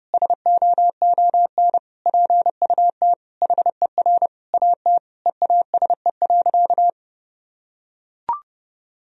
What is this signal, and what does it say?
Morse code, heard as a keyed tone, about 30 wpm, 700 Hz. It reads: SOON PUT HER AT EASE.
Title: Sentences from Top 1000 Words - Review Part 1 - 30wpm